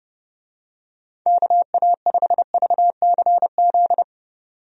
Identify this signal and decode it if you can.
Morse code, heard as a keyed tone, about 32 wpm, 700 Hz. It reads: KA5VCZ
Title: USA Call Signs - 6 Characters - 30wpm